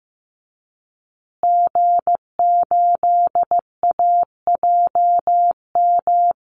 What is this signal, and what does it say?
Morse code, heard as a keyed tone, about 15 wpm, 700 Hz. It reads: G8AJM
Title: Call Signs Repeated 2x - 15wpm